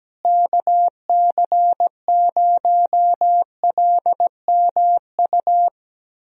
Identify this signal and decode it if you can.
Morse code, heard as a keyed tone, about 17 wpm, 700 Hz. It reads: KC0LMU
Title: USA Call Signs - 6 Characters - Repeated 3x - 17wpm